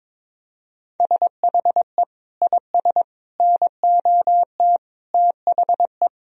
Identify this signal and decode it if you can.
Morse code, heard as a keyed tone, about 22 wpm, 700 Hz. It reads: SHE IS NOT THE
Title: Sentences from Top 1000 Words - Review Part 2 - 22wpm